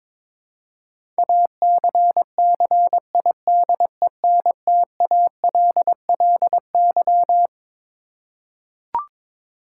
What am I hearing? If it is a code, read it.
Morse code, heard as a keyed tone, about 22 wpm, 700 Hz. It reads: ACCIDENTALLY
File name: Twelve Letter Words - 22wpm